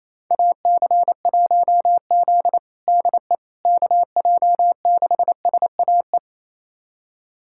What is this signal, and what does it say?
Morse code, heard as a keyed tone, about 28 wpm, 700 Hz. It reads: AC1Z DE KJ6SAE
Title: USA Callsign DE Callsign 28wpm